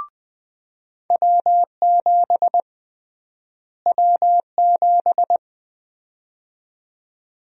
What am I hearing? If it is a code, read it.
Morse code, heard as a keyed tone, about 20 wpm, 700 Hz. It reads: W7 W7
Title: USA Call Sign Prefixes - Repeated 2x - 20wpm